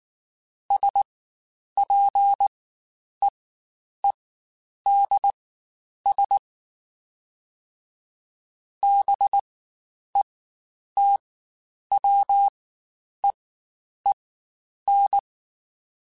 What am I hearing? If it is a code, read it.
Morse code, heard as a keyed tone, about 19 wpm, spaced as for 9 wpm, 800 Hz. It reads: SPEEDS BETWEEN
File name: Spaced Characters / Spaced Characters-mp3s / Weather 4 21 sp @18